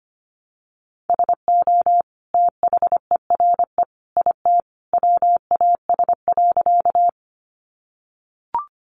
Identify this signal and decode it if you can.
Morse code, heard as a keyed tone, about 25 wpm, 700 Hz. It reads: SO THERE IT WAS.
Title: Sentences from Top 100 Words 25wpm